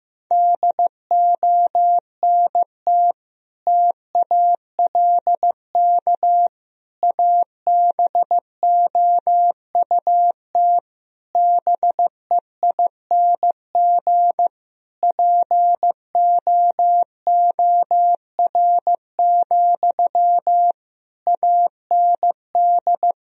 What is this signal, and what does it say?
Morse code, heard as a keyed tone, about 15 wpm, 700 Hz. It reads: DONT TALK ABOUT BEING POOR, AND
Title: Sentences from Top 1000 Words - Review Part 4 - 15wpm